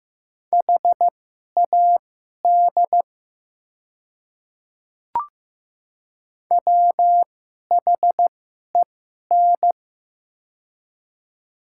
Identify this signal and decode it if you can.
Morse code, heard as a keyed tone, about 15 wpm, spaced as for 11 wpm, 700 Hz. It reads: HAD WHEN
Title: 2x Character Spacing - Top 100 Words - 15wpm